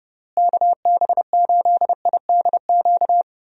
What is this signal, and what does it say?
Morse code, heard as a keyed tone, about 32 wpm, 700 Hz. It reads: KB8IDQ
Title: USA Call Signs - 6 Characters - Repeated 3x - 30wpm